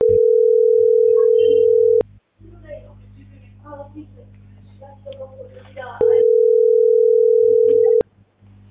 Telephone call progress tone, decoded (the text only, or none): ringback tone